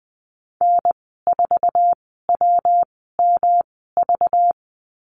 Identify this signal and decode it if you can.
Morse code, heard as a keyed tone, about 20 wpm, spaced as for 14 wpm, 700 Hz. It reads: N4WMV
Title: 207 Lesson 33g - Letter j - Call Signs 20wpm